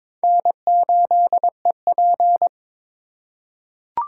Morse code, heard as a keyed tone, about 22 wpm, 700 Hz. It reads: N8EP
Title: USA Call Signs - 2x Character Spacing - 22wpm